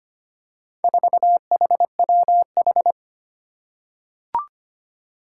Morse code, heard as a keyed tone, about 25 wpm, 700 Hz. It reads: 4HWH
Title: Four Challenging Letter-Number Combinations 25wpm